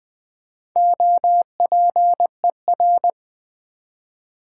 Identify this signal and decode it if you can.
Morse code, heard as a keyed tone, about 20 wpm, 700 Hz. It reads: OPER
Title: Build A Word - 1 Letter At A Time - 5-Letter Words - 20wpm